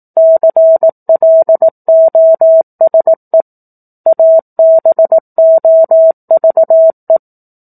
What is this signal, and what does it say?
Morse code, heard as a keyed tone, about 18 wpm, 650 Hz. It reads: CLOSE ABOVE
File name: 04-Chapter-04 18 WPM